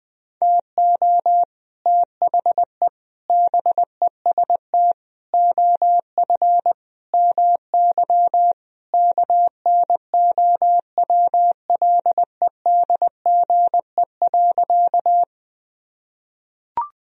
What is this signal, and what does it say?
Morse code, heard as a keyed tone, about 20 wpm, 700 Hz. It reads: TO THE BEST OF MY KNOWLEDGE.
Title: Sentences from Top 1000 Words - Review Part 3 - 20wpm